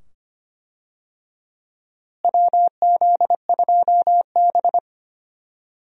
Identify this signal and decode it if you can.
Morse code, heard as a keyed tone, about 25 wpm, 700 Hz. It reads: WZ2B
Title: USA Call Signs - 4 Characters - Repeated 2x - 25wpm